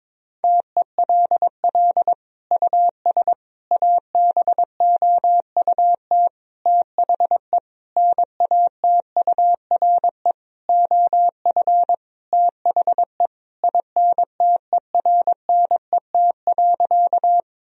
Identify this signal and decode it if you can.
Morse code, heard as a keyed tone, about 22 wpm, 700 Hz. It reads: TELL US ABOUT THE NATURE OF THE INTERNET.
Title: Sentences from Top 1000 Words - Review Part 4 - 22wpm